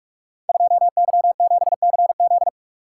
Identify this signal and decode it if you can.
Morse code, heard as a keyed tone, about 42 wpm, 700 Hz. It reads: 2Y7CZ
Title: Code Groups - 5 Letter:Numbers - 45wpm